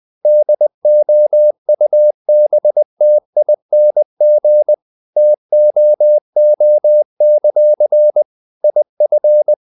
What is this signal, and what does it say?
Morse code, heard as a keyed tone, about 20 wpm, 600 Hz. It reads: DOUBTING TOO; IF
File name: CW / Practice / If20